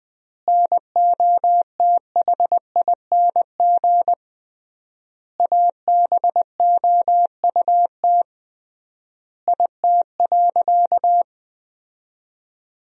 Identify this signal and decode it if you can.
Morse code, heard as a keyed tone, about 21 wpm, 700 Hz. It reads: NOTHING ABOUT IT.
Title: 3x Word-Spacing - Sentences from Top 500 Words - 20wpm